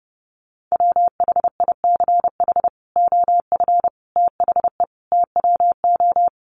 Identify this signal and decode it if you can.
Morse code, heard as a keyed tone, about 30 wpm, 700 Hz. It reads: WHICH OF THE TWO
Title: Sentences from Top 1300 Words 30wpm